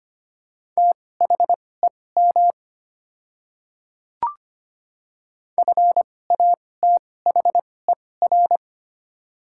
Morse code, heard as a keyed tone, about 25 wpm, spaced as for 18 wpm, 700 Hz. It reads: THEM FATHER
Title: 2x Character Spacing - Top 200 Words - 25wpm